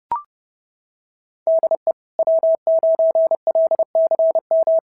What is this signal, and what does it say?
Morse code, heard as a keyed tone, about 31 wpm, 690 Hz. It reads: DE W9LCM
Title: SOTA QSO Practice 30wpm